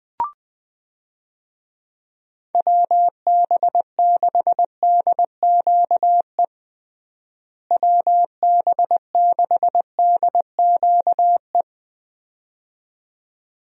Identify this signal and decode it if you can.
Morse code, heard as a keyed tone, about 19 wpm, 700 Hz. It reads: WB6DQE WB6DQE
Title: USA Call Signs - 6 Characters - Repeated 2x - 20wpm